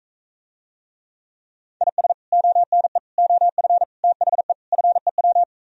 Morse code, heard as a keyed tone, about 45 wpm, 700 Hz. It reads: IS ONE OF THE FEW